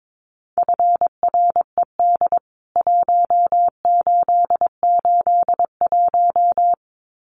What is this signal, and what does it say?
Morse code, heard as a keyed tone, about 22 wpm, 700 Hz. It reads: FRED 1881
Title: CWT Exchange 22wpm